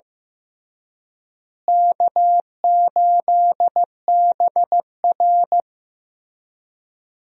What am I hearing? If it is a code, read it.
Morse code, heard as a keyed tone, about 15 wpm, 700 Hz. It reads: K8BR
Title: USA Call Signs - 4 Characters - Repeated 3x - 15wpm